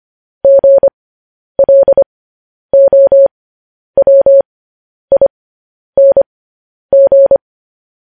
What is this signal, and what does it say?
Morse code, heard as a keyed tone, about 25 wpm, spaced as for 10 wpm, 550 Hz. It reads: GLOWING
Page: news-headlines-in-morse-code-at-10-wpm